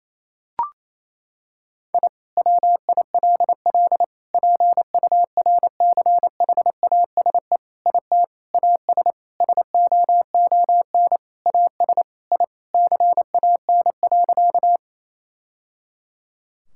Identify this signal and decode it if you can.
Morse code, heard as a keyed tone, about 28 wpm, 700 Hz. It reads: I WILL PURCHASE IT AS SOON AS I CAN.
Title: Sentences from Top 1000 Words - Review Part 5 - 28wpm